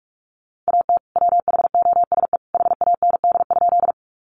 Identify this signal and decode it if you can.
Morse code, heard as a keyed tone, about 44 wpm, 700 Hz. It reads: AT WHOSE HAND?
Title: Sentences from Top 1200 Words 45wpm